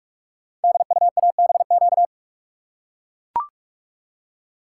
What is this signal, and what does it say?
Morse code, heard as a keyed tone, about 44 wpm, 700 Hz. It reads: DUABQ
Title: Code Groups - 5 Letters - 45wpm